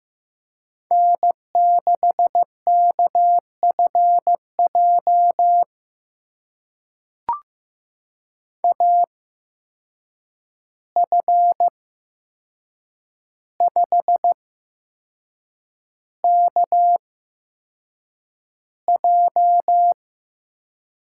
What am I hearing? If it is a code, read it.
Morse code, heard as a keyed tone, about 15 wpm, spaced as for 6 wpm, 700 Hz. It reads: N6KFJ AF5KJ
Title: USA Call Signs - 5 Characters - 8x Character Spacing - 15wpm